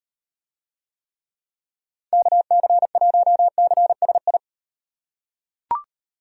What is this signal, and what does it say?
Morse code, heard as a keyed tone, about 38 wpm, 700 Hz. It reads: KC1CSI